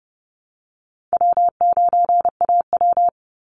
Morse code, heard as a keyed tone, about 30 wpm, 700 Hz. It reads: W9AW
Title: Easy Call Signs Repeated 3x - 30wpm